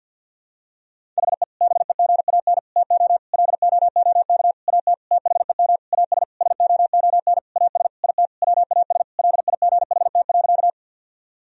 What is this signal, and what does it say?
Morse code, heard as a keyed tone, about 51 wpm, 700 Hz. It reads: HE BEGAN TO LOOK AT THEM AS SOON AS IT WAS LIGHT.
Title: Sentences from Top 1000 Words - Review Part 1 - 50wpm